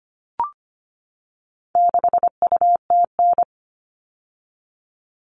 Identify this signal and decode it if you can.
Morse code, heard as a keyed tone, about 25 wpm, 700 Hz. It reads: 6UTN
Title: All Cut Numbers - 4-Digit 25wpm